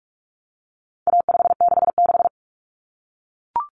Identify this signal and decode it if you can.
Morse code, heard as a keyed tone, about 44 wpm, 700 Hz. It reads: A566
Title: Common Cut Numbers - 4-Digit 45wpm